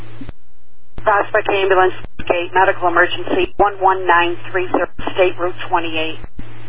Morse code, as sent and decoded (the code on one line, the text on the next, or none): none